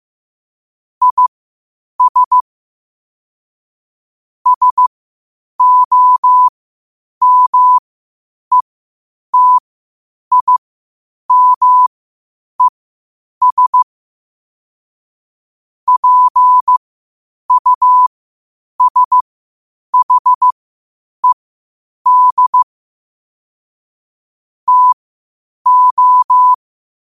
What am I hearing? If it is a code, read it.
Morse code, heard as a keyed tone, about 15 wpm, spaced as for 8 wpm, 1000 Hz. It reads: IS SOMETIMES PUSHED TO
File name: qotdfiles / 8wpm